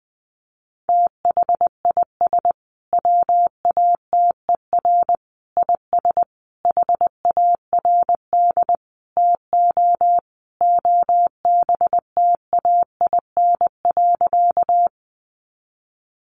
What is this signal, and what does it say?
Morse code, heard as a keyed tone, about 20 wpm, 700 Hz. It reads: THIS WATER IS HARD TO OBTAIN.
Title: Sentences from Top 1300 Words 20wpm